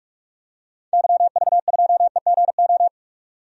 Morse code, heard as a keyed tone, about 42 wpm, 700 Hz. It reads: YV2EGO
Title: DXCC-100 Call Signs - 6 Characters - 45wpm